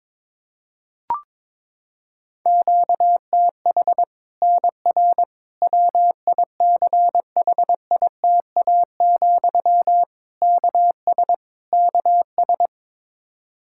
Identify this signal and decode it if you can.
Morse code, heard as a keyed tone, about 22 wpm, 700 Hz. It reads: QTH NR WICHITA, KS KS
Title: Practice QSO - Exchange 1 - Beginning - 22wpm